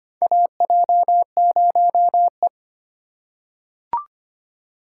Morse code, heard as a keyed tone, about 25 wpm, 700 Hz. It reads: AJ0E
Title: USA Call Signs - 4 Characters - 3x Character Spacing - 25wpm